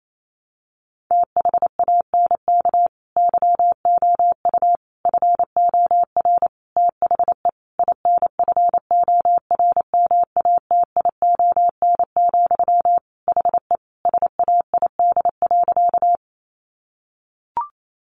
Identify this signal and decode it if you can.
Morse code, heard as a keyed tone, about 28 wpm, 700 Hz. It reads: THANK YOU FOR THE INFORMATION, HE SAID.
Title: Sentences from Top 600 Words 28wpm